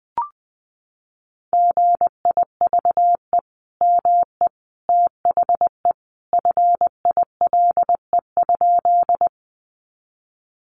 Morse code, heard as a keyed tone, about 19 wpm, 700 Hz. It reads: GIVE ME THE FILE?
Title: Sentences from Top 1200 Words 20wpm